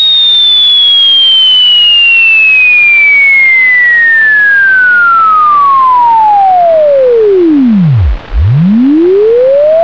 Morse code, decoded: none